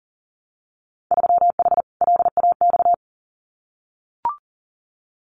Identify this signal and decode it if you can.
Morse code, heard as a keyed tone, about 39 wpm, 700 Hz. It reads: 3H LAX
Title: Winter Field Day Exchanges 40wpm